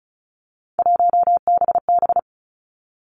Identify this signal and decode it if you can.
Morse code, heard as a keyed tone, about 36 wpm, 700 Hz. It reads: 1BB
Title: Three Challenging Letter-Number Combinations 35wpm